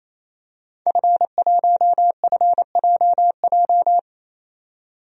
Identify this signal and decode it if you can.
Morse code, heard as a keyed tone, about 28 wpm, 700 Hz. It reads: F1FJJ
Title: DXCC-100 Call Signs - 5 Characters - 28wpm